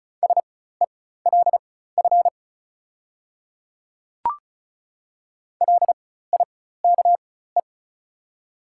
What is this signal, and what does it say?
Morse code, heard as a keyed tone, about 36 wpm, spaced as for 16 wpm, 690 Hz. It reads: SELF LIKE
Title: 4x Character Spacing - Top 300 Words - 35wpm